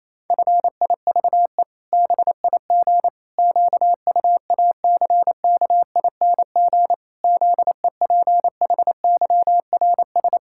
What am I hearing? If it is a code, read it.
Morse code, heard as a keyed tone, about 28 wpm, 700 Hz. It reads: FIVE BIG QUACKING ZEPHYRS
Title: Pangram Challenge v2 - 28wpm